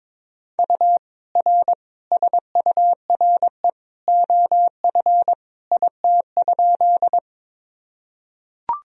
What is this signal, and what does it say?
Morse code, heard as a keyed tone, about 22 wpm, 700 Hz. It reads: U R SURE OF IT?
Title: Sentences from Top 300 Words and Abbreviations 22wpm